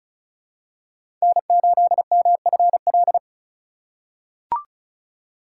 Morse code, heard as a keyed tone, about 34 wpm, 700 Hz. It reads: N8MFL
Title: USA Call Signs - 3x Character Spacing - 35wpm